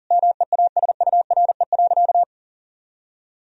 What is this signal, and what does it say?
Morse code, heard as a keyed tone, about 40 wpm, 700 Hz. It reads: MEASURE.
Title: Sentences from Top 1000 Words - Review Part 3 - 40wpm